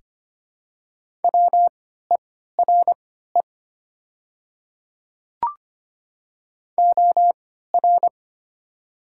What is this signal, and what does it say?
Morse code, heard as a keyed tone, about 25 wpm, spaced as for 14 wpm, 700 Hz. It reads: WERE OR
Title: 3x Character Spacing - Top 100 Words - 25wpm